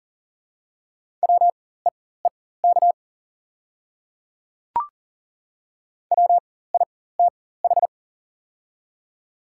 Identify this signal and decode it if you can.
Morse code, heard as a keyed tone, about 37 wpm, spaced as for 18 wpm, 700 Hz. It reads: WEEK WITH
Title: 4x Character Spacing - Top 500 Words - 40wpm